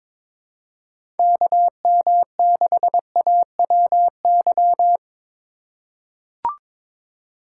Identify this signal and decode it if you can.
Morse code, heard as a keyed tone, about 22 wpm, 700 Hz. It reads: KM6AWY
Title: USA Call Signs - 6 Characters - 3x Character Spacing - 22wpm